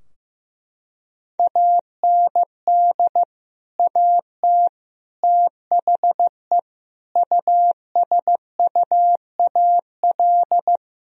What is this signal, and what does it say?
Morse code, heard as a keyed tone, about 16 wpm, 700 Hz. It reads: AND AT THE USUAL
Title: Sentences from Top 1000 Words - Review Part 2 - 15wpm